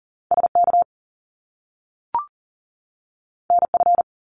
Morse code, heard as a keyed tone, about 38 wpm, 700 Hz. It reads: SK NF
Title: Top Two Letter Combinations in Words 40wpm